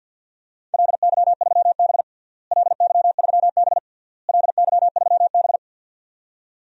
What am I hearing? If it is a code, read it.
Morse code, heard as a keyed tone, about 52 wpm, 700 Hz. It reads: LY3B LY3B LY3B
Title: DXCC-100 Call Signs - 4 Characters - Repeated 3x - 50wpm